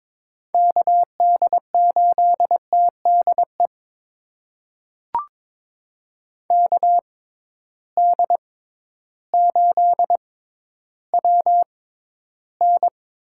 Morse code, heard as a keyed tone, about 22 wpm, spaced as for 12 wpm, 700 Hz. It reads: KD8TDE KD8WN
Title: USA Call Signs - 6 Characters - 6x Character Spacing - 22wpm